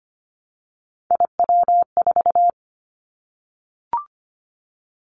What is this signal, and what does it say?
Morse code, heard as a keyed tone, about 25 wpm, 690 Hz. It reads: IW4
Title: Callsign Prefixes for Top 50 DXCC - 25wpm